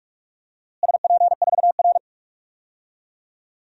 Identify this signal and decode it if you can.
Morse code, heard as a keyed tone, about 46 wpm, 700 Hz. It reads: SP4R
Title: DXCC-100 Call Signs - 4 Characters - Repeated 2x - 45wpm